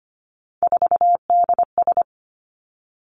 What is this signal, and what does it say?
Morse code, heard as a keyed tone, about 25 wpm, 700 Hz. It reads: 4DS
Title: Three Challenging Letter-Number Combinations 25wpm